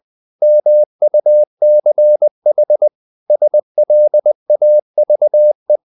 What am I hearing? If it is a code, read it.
Morse code, heard as a keyed tone, about 20 wpm, 600 Hz. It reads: MUCH SLAVE
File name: the old man and the sea part 2 20wpm